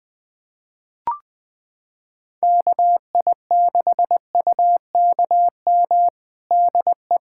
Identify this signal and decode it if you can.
Morse code, heard as a keyed tone, about 20 wpm, 700 Hz. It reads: KI6UKM DE
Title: USA Callsign DE Callsign - 6 Characters - 20wpm